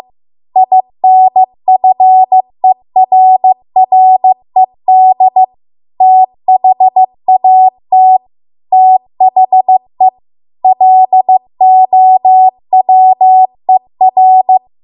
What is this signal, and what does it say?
Morse code, heard as a keyed tone, about 15 wpm, 750 Hz. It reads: INFERRED THAT THE LOWER